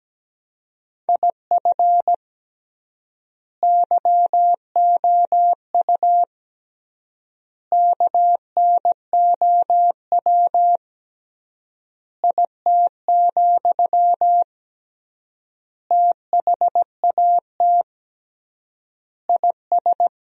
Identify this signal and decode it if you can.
Morse code, heard as a keyed tone, about 17 wpm, 700 Hz. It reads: IF YOU KNOW IT, THAT IS
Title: 3x Word-Spacing - Sentences from Top 400 Words - 17wpm